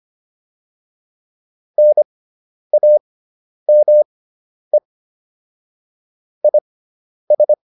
Morse code, heard as a keyed tone, about 25 wpm, spaced as for 10 wpm, 600 Hz. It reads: NAME IS
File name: QSO310 10